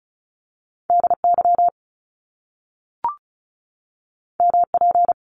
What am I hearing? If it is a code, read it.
Morse code, heard as a keyed tone, about 36 wpm, 700 Hz. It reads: DY MP
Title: Top Two Letter Combinations in Words 35wpm